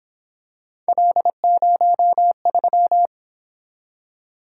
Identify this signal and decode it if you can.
Morse code, heard as a keyed tone, about 27 wpm, 700 Hz. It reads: L03
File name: Speed-Racing - Three Letter-Number Combinations - 22wpm